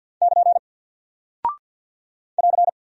Morse code, heard as a keyed tone, about 48 wpm, 700 Hz. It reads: <KN> <AR>
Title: Single Character (Letter:Number:Punctuation:Prosign) Alt 50wpm